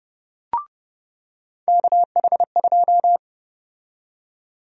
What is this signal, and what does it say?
Morse code, heard as a keyed tone, about 30 wpm, 700 Hz. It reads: KH2
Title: USA Call Sign Prefixes - 30wpm